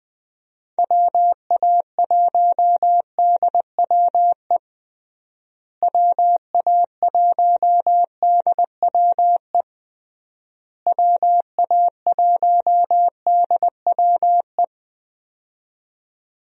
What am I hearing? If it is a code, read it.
Morse code, heard as a keyed tone, about 20 wpm, 700 Hz. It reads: WA1DWE WA1DWE WA1DWE
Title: USA Call Signs - 6 Characters - Repeated 3x - 20wpm